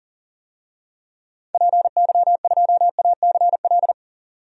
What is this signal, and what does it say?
Morse code, heard as a keyed tone, about 40 wpm, 700 Hz. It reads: PY2ACL
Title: DXCC-100 Call Signs - 6 Characters - 40wpm